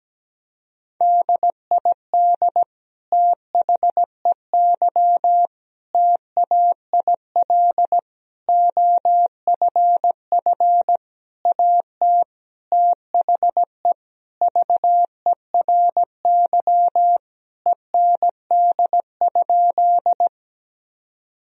Morse code, heard as a keyed tone, about 17 wpm, 700 Hz. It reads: DID THEY TAIL OFF AT THE VERY END?
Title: Sentences from Top 1000 Words - Review Part 3 - 17wpm